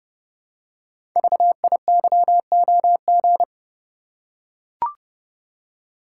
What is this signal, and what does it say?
Morse code, heard as a keyed tone, about 29 wpm, 700 Hz. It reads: VIYOG
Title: Code Groups - 5 Letters - 30wpm